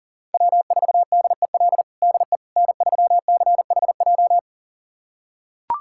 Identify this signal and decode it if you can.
Morse code, heard as a keyed tone, about 40 wpm, 700 Hz. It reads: W4DEL DE N3CHJ
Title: USA Callsign DE Callsign - 5 Characters - 40wpm